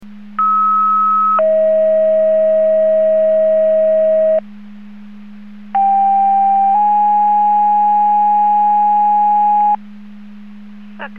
Two-tone page, heard as a one-second tone, then a three-second tone, about 1290 then 640 Hz, about 790 then 830 Hz. It is heard from 0.4 to 9.8 s.